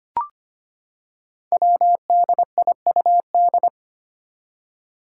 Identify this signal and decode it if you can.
Morse code, heard as a keyed tone, about 25 wpm, 700 Hz. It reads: WDIUD
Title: Code Groups - 5 Letter:Numbers - 25wpm